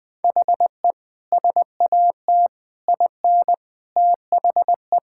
Morse code, heard as a keyed tone, about 20 wpm, 700 Hz. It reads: HE SAT IN THE
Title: Sentences with 3-Word Phrases - 20wpm